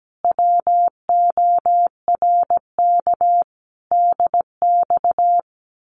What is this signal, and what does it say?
Morse code, heard as a keyed tone, about 17 wpm, 700 Hz. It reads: WORK DX
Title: Common QSO Phrases Encore 17wpm